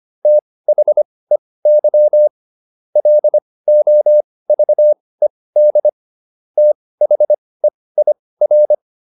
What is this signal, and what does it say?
Morse code, heard as a keyed tone, about 25 wpm, spaced as for 18 wpm, 600 Hz. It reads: THEY LOVED THEIR